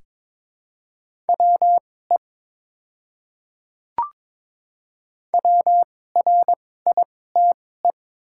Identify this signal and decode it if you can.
Morse code, heard as a keyed tone, about 22 wpm, spaced as for 16 wpm, 700 Hz. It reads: WE WRITE
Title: 2x Character Spacing - Top 100 Words - 22wpm